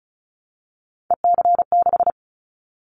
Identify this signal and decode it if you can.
Morse code, heard as a keyed tone, about 36 wpm, 700 Hz. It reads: EC6
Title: Callsign Prefixes for Top 100 DXCC - 35wpm